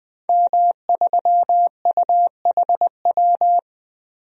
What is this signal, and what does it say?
Morse code, heard as a keyed tone, about 20 wpm, 700 Hz. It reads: M3UHW
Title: DXCC-100 Call Signs - 5 Characters - 20wpm